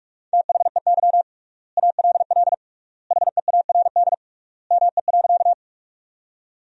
Morse code, heard as a keyed tone, about 41 wpm, 700 Hz. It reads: THEY ALL HEARD ME.
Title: 3x Word-Spacing - Sentences from Top 400 Words - 45wpm